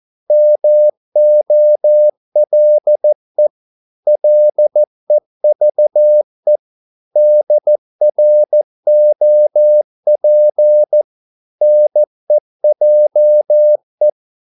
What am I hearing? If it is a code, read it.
Morse code, heard as a keyed tone, about 14 wpm, 600 Hz. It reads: MOLE LEVE DROP NEJE
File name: Korte ord DK 14wpm